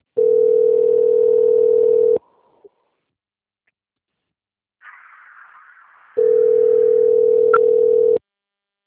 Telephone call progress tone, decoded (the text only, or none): ringback tone